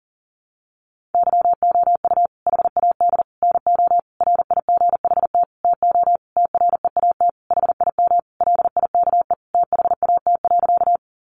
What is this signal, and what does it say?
Morse code, heard as a keyed tone, about 40 wpm, 700 Hz. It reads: YOU HAD NO RIGHT TO TREAT HIM LIKE THAT.
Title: Sentences from Top 1300 Words 40wpm